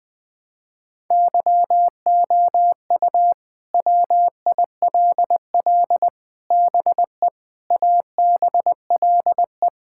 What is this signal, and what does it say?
Morse code, heard as a keyed tone, about 20 wpm, 700 Hz. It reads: YOU WILL BE ABLE
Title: Sentences with 4-Word Phrases - 20wpm